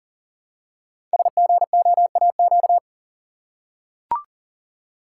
Morse code, heard as a keyed tone, about 38 wpm, 700 Hz. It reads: SGOAQ
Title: Code Groups - 5 Letters - 40wpm